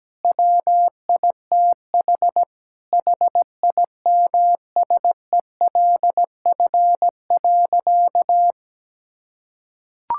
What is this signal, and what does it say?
Morse code, heard as a keyed tone, about 17 wpm, 700 Hz. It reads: WITH HIMSELF.
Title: Sentences from Top 1600 Words 17wpm